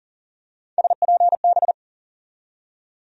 Morse code, heard as a keyed tone, about 40 wpm, 700 Hz. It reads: SPB
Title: USA Call Sign Postfixes - 3 Letters 40wpm